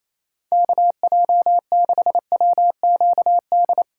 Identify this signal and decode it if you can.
Morse code, heard as a keyed tone, about 28 wpm, 700 Hz. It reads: KJ6WQD
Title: USA Call Signs - 8x Character Spacing - 28wpm